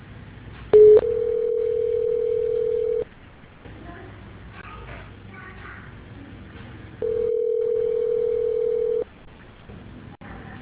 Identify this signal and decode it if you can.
Ringback tone (North America), 440 + 480 Hz.